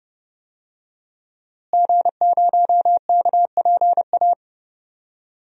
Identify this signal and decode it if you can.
Morse code, heard as a keyed tone, about 30 wpm, 700 Hz. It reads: G0KPA
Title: DXCC-100 Call Signs - 5 Characters - Repeated 2x - 30wpm